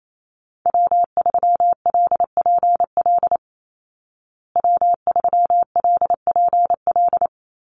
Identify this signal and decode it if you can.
Morse code, heard as a keyed tone, about 28 wpm, 700 Hz. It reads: W3LPL W3LPL
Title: Call Signs Repeated 3x - 28wpm